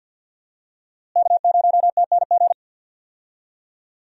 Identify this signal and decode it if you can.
Morse code, heard as a keyed tone, about 52 wpm, 700 Hz. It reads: K0TNG
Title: USA Call Signs - 5 Characters - Repeated 2x - 50wpm